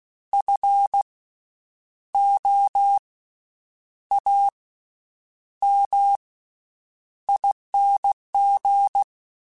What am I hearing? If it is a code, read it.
Morse code, heard as a keyed tone, about 16 wpm, spaced as for 8 wpm, 800 Hz. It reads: FOAMING
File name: Special INGs / Special INGs-mp3s / VerySlowToFaster Ing @15